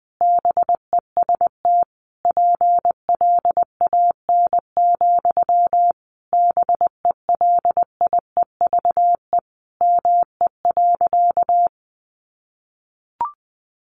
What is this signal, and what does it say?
Morse code, heard as a keyed tone, about 20 wpm, 700 Hz. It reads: BEST PLAN, BELIEVE ME.
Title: Sentences from Top 1000 Words 20wpm